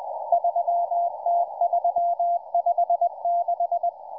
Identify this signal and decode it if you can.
Morse code, heard as a keyed tone, about 20 wpm, 690 Hz. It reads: 3T356